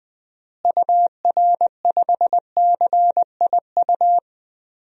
Morse code, heard as a keyed tone, about 20 wpm, 700 Hz. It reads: UR5CIU
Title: DXCC-100 Call Signs - 6 Characters - Repeated 3x - 20wpm